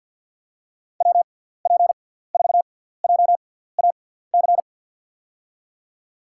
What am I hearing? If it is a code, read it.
Morse code, heard as a keyed tone, about 53 wpm, spaced as for 17 wpm, 700 Hz. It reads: WP4JAC
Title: USA Call Signs - 6 Characters - 6x Character Spacing - 50wpm